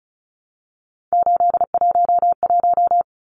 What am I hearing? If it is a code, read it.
Morse code, heard as a keyed tone, about 34 wpm, 700 Hz. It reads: 811
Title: Numbers - 3-Digit 35wpm